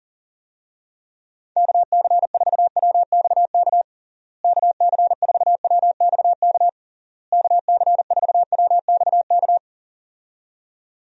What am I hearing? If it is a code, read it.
Morse code, heard as a keyed tone, about 40 wpm, 700 Hz. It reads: KC4WXK KC4WXK KC4WXK
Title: USA Call Signs - 6 Characters - Repeated 3x - 40wpm